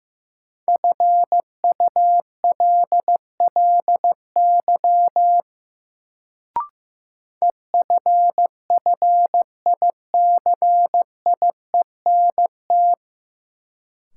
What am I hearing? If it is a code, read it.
Morse code, heard as a keyed tone, about 15 wpm, 700 Hz. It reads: FULLY EFFICIENT
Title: Sentences from Top 1300 Words - New Word Drill Alt - 15wpm